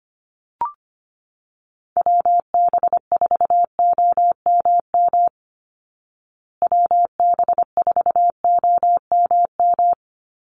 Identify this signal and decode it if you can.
Morse code, heard as a keyed tone, about 25 wpm, 700 Hz. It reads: WB4OMM WB4OMM
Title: Call Signs Repeated 2x - 25wpm